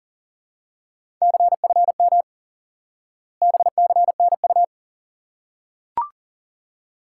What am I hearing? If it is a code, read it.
Morse code, heard as a keyed tone, about 40 wpm, 700 Hz. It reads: CFM BCNU
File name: IQE Course 3 - Lesson 89 - YAGI - Two Elements - 40wpm